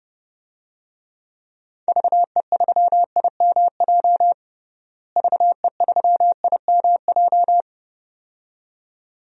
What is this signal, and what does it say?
Morse code, heard as a keyed tone, about 32 wpm, 700 Hz. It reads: VE3IMJ VE3IMJ
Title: DXCC-100 Call Signs - 6 Characters - Repeated 2x - 30wpm